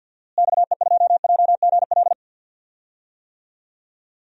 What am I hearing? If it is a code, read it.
Morse code, heard as a keyed tone, about 50 wpm, 700 Hz. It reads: XE2JGL
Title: DXCC-100 Call Signs - 6 Characters - Repeated 3x - 50wpm